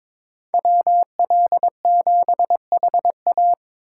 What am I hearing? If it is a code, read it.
Morse code, heard as a keyed tone, about 26 wpm, 700 Hz. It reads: WL7HA
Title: Speed Racing - USA 2x2 Call Signs 22wpm